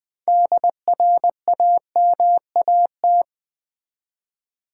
Morse code, heard as a keyed tone, about 20 wpm, 700 Hz. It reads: DRAMAT
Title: Build A Word - 6 Letters At A Time - 12-Letter Words - 20wpm